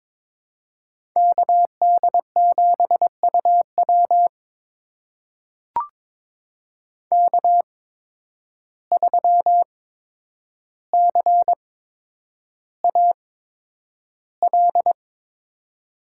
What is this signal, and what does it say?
Morse code, heard as a keyed tone, about 22 wpm, spaced as for 9 wpm, 700 Hz. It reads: KD7UW K3CAL
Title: USA Call Signs - 5 Characters - 8x Character Spacing - 22wpm